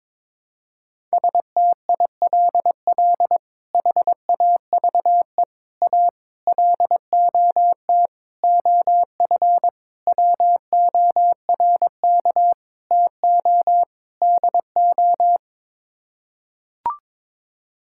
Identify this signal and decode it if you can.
Morse code, heard as a keyed tone, about 22 wpm, 700 Hz. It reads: STILL HAVE A LOT OF WORK TO DO
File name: practice / 22wpm / Sets of 8 Words - 22wpm